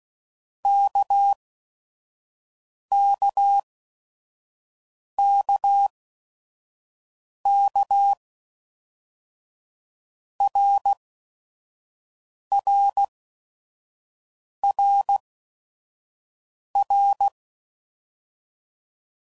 Morse code, heard as a keyed tone, about 16 wpm, spaced as for 5 wpm, 800 Hz. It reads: KKKK RRRR
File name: Basic Letter Groups By7Groups / RKPX / RKPX@5 f15